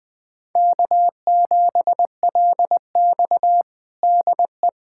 Morse code, heard as a keyed tone, about 20 wpm, 700 Hz. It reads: K7LX DE
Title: USA Callsign DE Callsign - 4 Characters - 20wpm